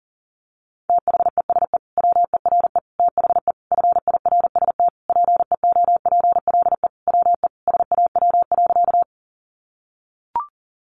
Morse code, heard as a keyed tone, about 40 wpm, 700 Hz. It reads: THESE WERE THE FIRST PEOPLE WE SAW.
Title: Sentences from Top 300 Words 40wpm